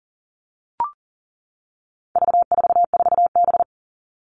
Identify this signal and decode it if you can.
Morse code, heard as a keyed tone, about 38 wpm, 700 Hz. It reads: V44B
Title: All Cut Numbers - 4-Digit 40wpm